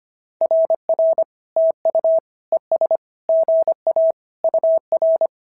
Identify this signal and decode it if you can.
Morse code, heard as a keyed tone, about 24 wpm, 650 Hz. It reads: RR TU ES GA UR
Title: SOTA QSO Practice 25wpm